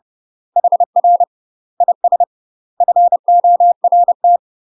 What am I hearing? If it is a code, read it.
Morse code, heard as a keyed tone, about 30 wpm, 700 Hz. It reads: HR IS FORT
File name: UK MASTER QSOs TRAINING / 30WPM / UK MASTER QSOs 0009